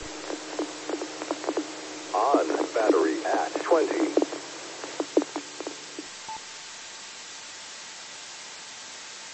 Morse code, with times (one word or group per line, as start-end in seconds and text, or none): none